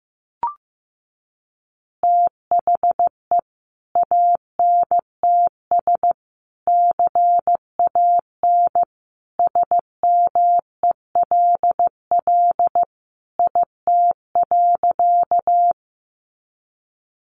Morse code, heard as a keyed tone, about 15 wpm, 700 Hz. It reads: THE ANTS CAN SMELL IT.
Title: Easy Phrases 15wpm